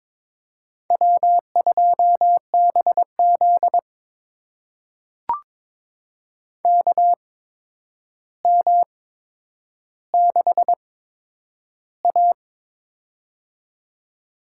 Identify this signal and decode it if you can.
Morse code, heard as a keyed tone, about 22 wpm, spaced as for 9 wpm, 700 Hz. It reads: W2BZ KM6A
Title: USA Call Signs - 4 Characters - 8x Character Spacing - 22wpm